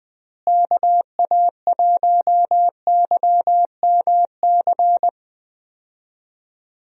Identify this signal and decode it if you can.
Morse code, heard as a keyed tone, about 21 wpm, 700 Hz. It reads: KA1YMC
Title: USA Call Signs - 6 Characters - Repeated 3x - 20wpm